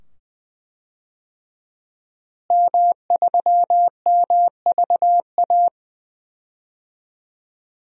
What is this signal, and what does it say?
Morse code, heard as a keyed tone, about 20 wpm, 700 Hz. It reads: M3MVA
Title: DXCC-100 Call Signs - 5 Characters - Repeated 2x - 20wpm